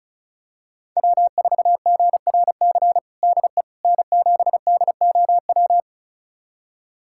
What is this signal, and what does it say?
Morse code, heard as a keyed tone, about 36 wpm, 700 Hz. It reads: W4GRC DE N7DOW
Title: USA Callsign DE Callsign - 5 Characters - 35wpm